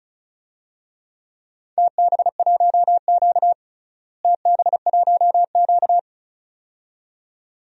Morse code, heard as a keyed tone, about 34 wpm, 700 Hz. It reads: TB1Q TB1Q
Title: DXCC-100 Call Signs - 4 Characters - Repeated 2x - 35wpm